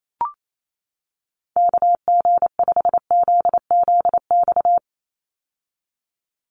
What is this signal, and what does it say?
Morse code, heard as a keyed tone, about 28 wpm, 700 Hz. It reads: KG5ZZX
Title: USA Call Signs 28wpm